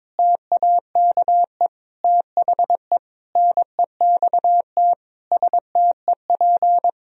Morse code, heard as a keyed tone, about 22 wpm, 700 Hz. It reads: TAKE THE NEXT STEP
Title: Sentences from Top 1000 Words - Review Part 2 - 22wpm